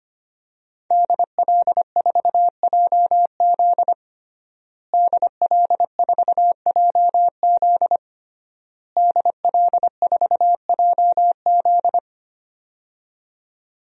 Morse code, heard as a keyed tone, about 25 wpm, 700 Hz. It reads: DL4JZ DL4JZ DL4JZ
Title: DXCC-100 Call Signs - 5 Characters - Repeated 3x - 25wpm